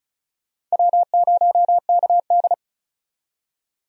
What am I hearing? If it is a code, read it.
Morse code, heard as a keyed tone, about 36 wpm, 700 Hz. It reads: W0KD
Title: USA Call Signs - 4 Characters - 35wpm